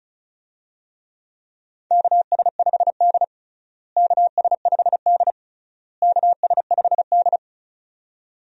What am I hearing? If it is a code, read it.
Morse code, heard as a keyed tone, about 36 wpm, 700 Hz. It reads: KS5D KS5D KS5D
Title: USA Call Signs - 4 Characters - Repeated 3x - 35wpm